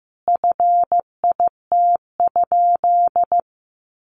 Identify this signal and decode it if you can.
Morse code, heard as a keyed tone, about 15 wpm, 700 Hz. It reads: FIT?
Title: Sentences from Top 800 Words 15wpm